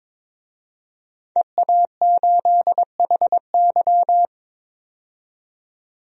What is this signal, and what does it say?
Morse code, heard as a keyed tone, about 22 wpm, 700 Hz. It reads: EA8HY
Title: DXCC-100 Call Signs - 5 Characters - 22wpm